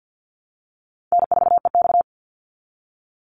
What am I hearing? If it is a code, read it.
Morse code, heard as a keyed tone, about 50 wpm, 700 Hz. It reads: N4EX
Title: NA SOTA Chaser Callsigns 50wpm